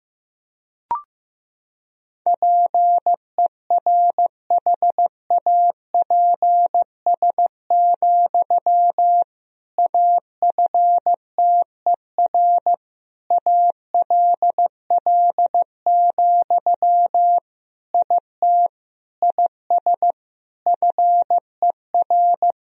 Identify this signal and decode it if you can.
Morse code, heard as a keyed tone, about 15 wpm, 700 Hz. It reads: PERHAPS, AFTER ALL, IT IS FER
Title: Sentences from Top 500 Words and Abbreviations 15wpm